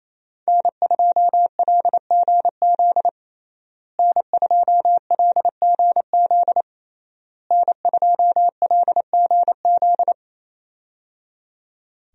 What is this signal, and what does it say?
Morse code, heard as a keyed tone, about 27 wpm, 700 Hz. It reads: N2LGZ N2LGZ N2LGZ
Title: USA Call Signs - 5 Characters - Repeated 3x - 28wpm